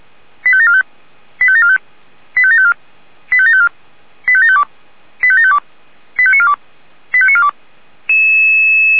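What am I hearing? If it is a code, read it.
Five-tone selective call (ZVEI), heard as quick runs of five tones, tones of about 80 ms. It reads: 85745 85745 85754 85754 85752 85752 85942 85942
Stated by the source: ZVEI five-tone call received by SDR.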